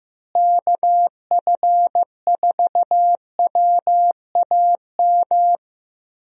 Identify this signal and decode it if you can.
Morse code, heard as a keyed tone, about 15 wpm, 700 Hz. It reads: KF4WAM
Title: USA Call Signs - 6 Characters - Repeated 2x - 15wpm